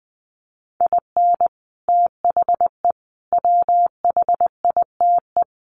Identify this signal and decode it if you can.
Morse code, heard as a keyed tone, about 20 wpm, 700 Hz. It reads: IN THE WHITE
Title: Sets of 4 Words 20wpm